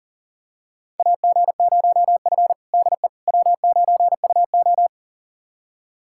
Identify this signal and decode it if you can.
Morse code, heard as a keyed tone, about 40 wpm, 700 Hz. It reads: AG0F DE W9UO
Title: USA Callsign DE Callsign - 4 Characters - 40wpm